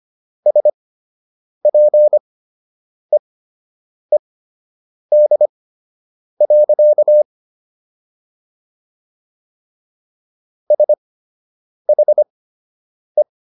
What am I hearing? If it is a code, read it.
Morse code, heard as a keyed tone, about 25 wpm, spaced as for 8 wpm, 600 Hz. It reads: SPEED. SHE